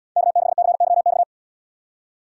Morse code, heard as a keyed tone, about 69 wpm, 700 Hz. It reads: K6CYX